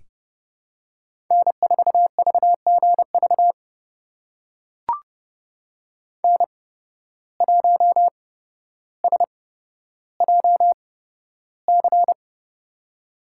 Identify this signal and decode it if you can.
Morse code, heard as a keyed tone, about 29 wpm, spaced as for 13 wpm, 700 Hz. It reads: N4VGV N1SJC
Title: USA Call Signs - 5 Characters - 8x Character Spacing - 30wpm